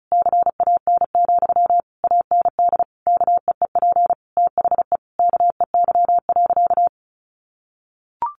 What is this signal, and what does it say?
Morse code, heard as a keyed tone, about 34 wpm, 700 Hz. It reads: CAN, AND KEEP THE KEY.
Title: Sentences from Top 1300 Words 35wpm